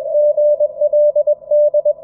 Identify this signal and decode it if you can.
Morse code, heard as a keyed tone, about 22 wpm, 590 Hz. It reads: GLD
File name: F5NZY-QRP-5W-HS0ZME-20M-06-01-2019